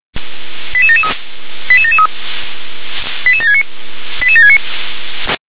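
Five-tone selective call (ZVEI), heard as quick runs of five tones, tones of about 70 ms. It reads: 88773 88773 88769 88769
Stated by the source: Five-tone ZVEI selective call 88783 received on SDR.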